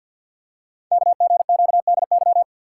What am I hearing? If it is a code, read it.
Morse code, heard as a keyed tone, about 53 wpm, 700 Hz. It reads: KGQDY
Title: Code Groups - 5 Letters - 50wpm